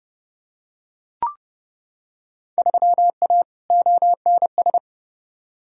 Morse code, heard as a keyed tone, about 32 wpm, 700 Hz. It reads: 3A ONS
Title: Field Day Exchanges - 30wpm